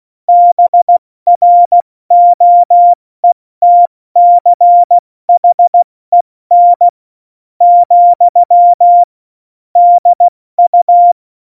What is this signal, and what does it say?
Morse code, heard as a keyed tone, about 16 wpm, spaced as for 13 wpm, 700 Hz. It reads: BROETCHEN, DU
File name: Morsekurs / Witze / Witz 11. September 2018, 70 F80